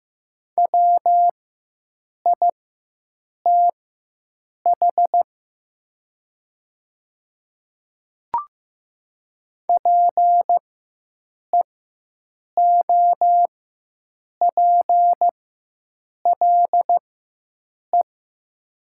Morse code, heard as a keyed tone, about 15 wpm, spaced as for 7 wpm, 700 Hz. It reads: WITH PEOPLE
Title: 4x Character Spacing - Top 100 Words - 15wpm